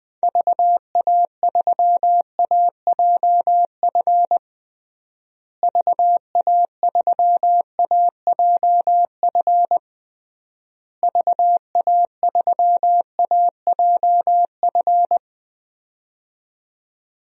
Morse code, heard as a keyed tone, about 21 wpm, 700 Hz. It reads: VA3AJF VA3AJF VA3AJF
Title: DXCC-100 Call Signs - 6 Characters - Repeated 3x - 20wpm